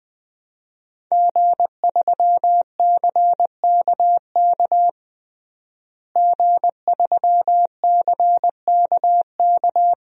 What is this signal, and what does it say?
Morse code, heard as a keyed tone, about 20 wpm, 700 Hz. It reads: G3CKK G3CKK
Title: DXCC-100 Call Signs - 5 Characters - Repeated 2x - 20wpm